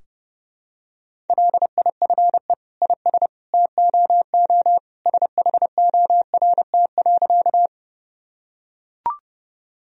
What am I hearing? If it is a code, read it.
Morse code, heard as a keyed tone, about 32 wpm, 700 Hz. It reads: LIFE IS TOO SHORT.
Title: Sentences from Top 1000 Words - Review Part 2 - 30wpm